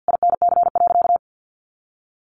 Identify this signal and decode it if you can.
Morse code, heard as a keyed tone, about 50 wpm, 700 Hz. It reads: INC.
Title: A Sound Of Thunder 50wpm